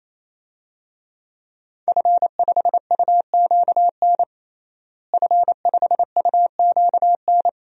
Morse code, heard as a keyed tone, about 28 wpm, 700 Hz. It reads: F5UQN F5UQN
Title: DXCC-100 Call Signs - 5 Characters - Repeated 3x - 28wpm